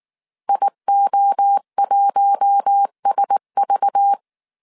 Morse code, heard as a keyed tone, about 19 wpm, 800 Hz. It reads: IO1SV
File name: CW / DET / 1 6 1500C S C 18wpm 2356